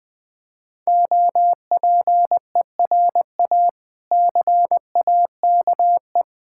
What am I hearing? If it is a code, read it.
Morse code, heard as a keyed tone, about 20 wpm, 700 Hz. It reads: OPERA CAKE
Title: Difficult Words with Hints - 20wpm